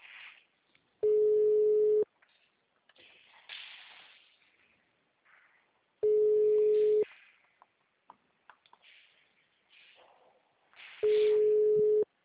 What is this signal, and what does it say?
Ringback tone (Europe, 425 Hz).
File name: out- -2002- - - .6946